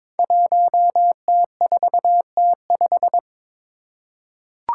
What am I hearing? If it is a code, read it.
Morse code, heard as a keyed tone, about 22 wpm, 700 Hz. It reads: 1T4T5
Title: Five Challenging Letter-Number Combinations 22wpm